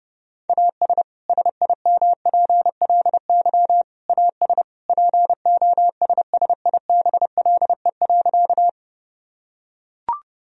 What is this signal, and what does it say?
Morse code, heard as a keyed tone, about 30 wpm, 700 Hz. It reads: AS SIMPLY AS POSSIBLE.
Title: Sentences from Top 1000 Words - Review Part 4 - 30wpm